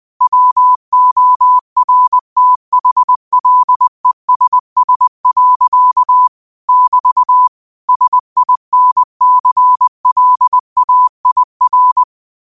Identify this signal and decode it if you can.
Morse code, heard as a keyed tone, about 20 wpm, 1000 Hz. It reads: WORTHLESS. <BT> SINCLAIR
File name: qotdfiles / 20wpm